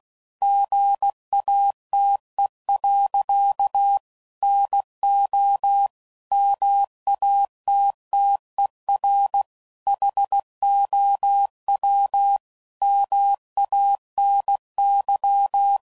Morse code, heard as a keyed tone, about 16 wpm, 800 Hz. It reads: GATE. NO MATTER HOW MANY
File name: CW Code Groups @15WPM / Technical Subjects / AND Gate